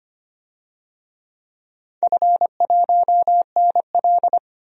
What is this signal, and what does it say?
Morse code, heard as a keyed tone, about 25 wpm, 700 Hz. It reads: F1NL
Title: DXCC-100 Call Signs - 4 Characters - 25wpm